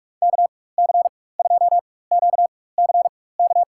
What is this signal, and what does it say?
Morse code, heard as a keyed tone, about 46 wpm, spaced as for 21 wpm, 700 Hz. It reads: KC2QCK
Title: USA Call Signs - 6 Characters - 4x Character Spacing - 45wpm